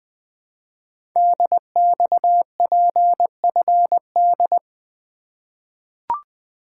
Morse code, heard as a keyed tone, about 20 wpm, 700 Hz. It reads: DXPFD
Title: Code Groups - 5 Letters - 20wpm